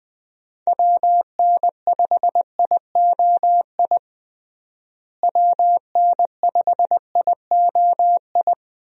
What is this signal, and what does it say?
Morse code, heard as a keyed tone, about 20 wpm, 700 Hz. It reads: WN5IOI WN5IOI
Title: USA Call Signs - 6 Characters - Repeated 2x - 20wpm